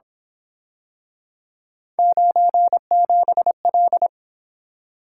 Morse code, heard as a keyed tone, about 28 wpm, 700 Hz. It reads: 97L